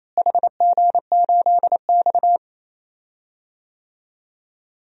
Morse code, heard as a keyed tone, about 28 wpm, 700 Hz. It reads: HG8X
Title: DXCC-100 Call Signs - 4 Characters - Repeated 3x - 28wpm